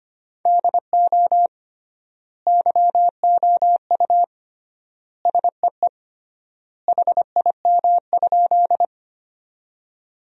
Morse code, heard as a keyed tone, about 25 wpm, 700 Hz. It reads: DO YOU SEE HIM?
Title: 3x Word-Spacing - Sentences from Top 100 Words - 25wpm